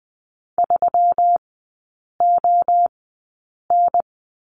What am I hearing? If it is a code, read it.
Morse code, heard as a keyed tone, about 20 wpm, spaced as for 8 wpm, 700 Hz. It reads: 3ON
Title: 121 Lesson 20e - Number 3 - Three Characters 20wpm